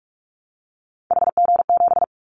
Morse code, heard as a keyed tone, about 44 wpm, 700 Hz. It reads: HG7
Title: Callsign Prefixes for Top 25 DXCC - Repeated 3x - 45wpm